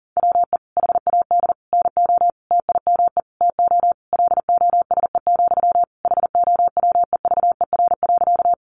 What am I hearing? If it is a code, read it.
Morse code, heard as a keyed tone, about 40 wpm, 700 Hz. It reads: WE HAD NO TIME TO LOSE, HOWEVER.
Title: Sentences from Top 1100 Words 40wpm